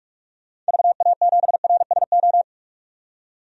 Morse code, heard as a keyed tone, about 43 wpm, 700 Hz. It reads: VA7RSO
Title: DXCC-100 Call Signs - 6 Characters - 45wpm